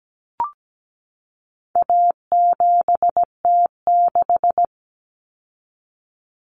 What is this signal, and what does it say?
Morse code, heard as a keyed tone, about 17 wpm, 700 Hz. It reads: A7T6
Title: Common Cut Numbers - 4-Digit 17wpm